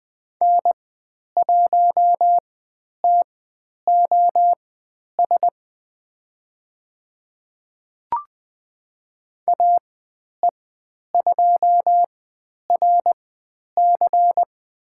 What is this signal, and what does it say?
Morse code, heard as a keyed tone, about 20 wpm, spaced as for 10 wpm, 700 Hz. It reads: N1TOS AE2RC
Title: 095 Lesson 16g - Letter c - Call Signs 20wpm